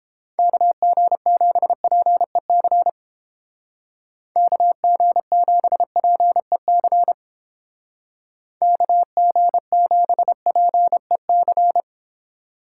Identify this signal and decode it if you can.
Morse code, heard as a keyed tone, about 29 wpm, 700 Hz. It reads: KG7PEC KG7PEC KG7PEC